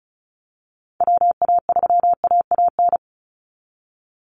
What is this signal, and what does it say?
Morse code, heard as a keyed tone, about 34 wpm, 700 Hz. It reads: WA3AAN
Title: Call Signs Repeated 3x - 35wpm